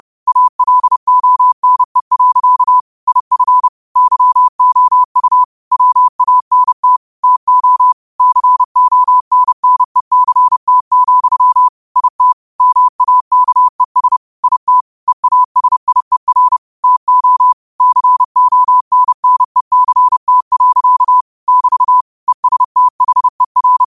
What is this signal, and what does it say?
Morse code, heard as a keyed tone, about 30 wpm, 1000 Hz. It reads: ALONE. IF YOU WANT TO CONNECT, IT MAKES IT EASIER TO CONNECT. <BT> ESTHER
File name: qotdfiles / 30wpm